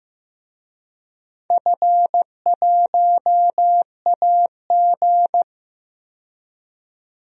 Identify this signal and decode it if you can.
Morse code, heard as a keyed tone, about 15 wpm, 700 Hz. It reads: F1AG
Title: DXCC-100 Call Signs - 4 Characters - 15wpm